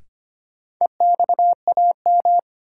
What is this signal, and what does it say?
Morse code, heard as a keyed tone, about 25 wpm, 700 Hz. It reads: EXAM
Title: Sentences from Top 1000 Words - Review Part 5 - New Word Drill - 25wpm